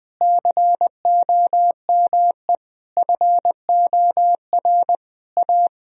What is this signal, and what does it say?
Morse code, heard as a keyed tone, about 21 wpm, 700 Hz. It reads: COME FOR A
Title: Sentences from Top 1000 Words - Review Part 1 - 20wpm